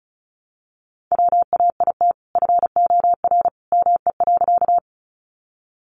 Morse code, heard as a keyed tone, about 34 wpm, 700 Hz. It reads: WAIT FOR ME.
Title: Sentences from Top 500 Words 35wpm